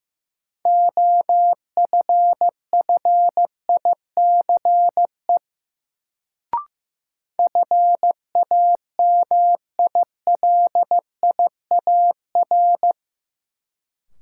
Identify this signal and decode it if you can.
Morse code, heard as a keyed tone, about 15 wpm, 700 Hz. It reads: OFFICE FAMILIAR
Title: Sentences from Top 1000 Words - New Word Drill - 15wpm